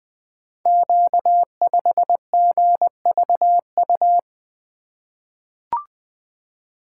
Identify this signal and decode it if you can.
Morse code, heard as a keyed tone, about 20 wpm, 700 Hz. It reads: Q5GVU
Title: Code Groups - 5 Letter:Numbers - 20wpm